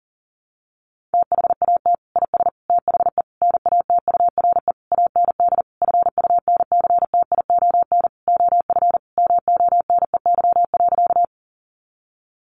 Morse code, heard as a keyed tone, about 37 wpm, 700 Hz. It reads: THAT IS THE NATURE AND FUNCTION OF MONEY.
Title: Sentences from Top 1100 Words 40wpm